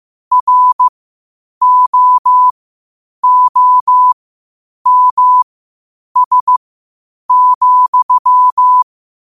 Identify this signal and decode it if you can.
Morse code, heard as a keyed tone, about 15 wpm, spaced as for 8 wpm, 1000 Hz. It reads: ROOMS,
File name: qotdfiles / 8wpm